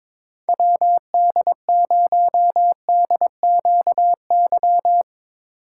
Morse code, heard as a keyed tone, about 25 wpm, 700 Hz. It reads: WD0DQY
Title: Speed Racing - USA 2x3 Call Signs 22wpm